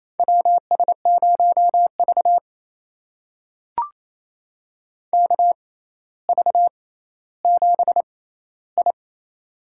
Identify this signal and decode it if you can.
Morse code, heard as a keyed tone, about 28 wpm, spaced as for 14 wpm, 700 Hz. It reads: WS0V KV7I
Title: USA Call Signs - 4 Characters - 6x Character Spacing - 28wpm